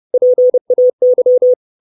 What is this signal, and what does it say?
Morse code, heard as a keyed tone, about 30 wpm, 500 Hz. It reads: PAY
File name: 632 Words 30WPM